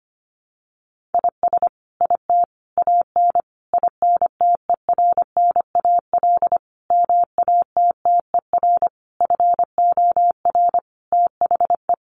Morse code, heard as a keyed tone, about 25 wpm, 700 Hz. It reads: IS IT AN INTERNAL MATTER FOR THE
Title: Sentences from Top 1400 Words 25wpm